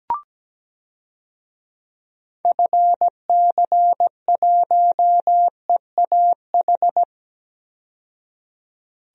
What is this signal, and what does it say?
Morse code, heard as a keyed tone, about 17 wpm, 700 Hz. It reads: FC1EAH
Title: DXCC-100 Call Signs - 6 Characters - 17wpm